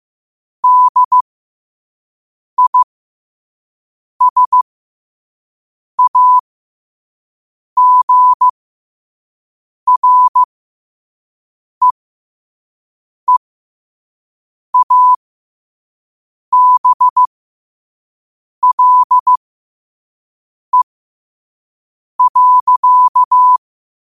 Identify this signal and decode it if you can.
Morse code, heard as a keyed tone, about 15 wpm, spaced as for 5 wpm, 1000 Hz. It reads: DISAGREEABLE.
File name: qotdfiles / 5wpm